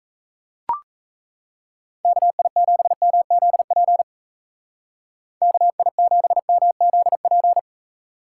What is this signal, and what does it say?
Morse code, heard as a keyed tone, about 35 wpm, 700 Hz. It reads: KI7MZP KI7MZP